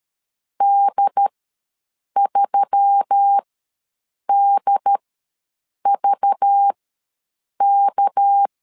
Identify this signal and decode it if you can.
Morse code, heard as a keyed tone, about 13 wpm, spaced as for 7 wpm, 800 Hz. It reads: D3DVK
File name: LEZIONI / 8 YJ45 / 8 YJ ws 12wpm 3342